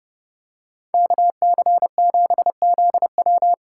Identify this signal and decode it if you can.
Morse code, heard as a keyed tone, about 30 wpm, 700 Hz. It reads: KC7ZW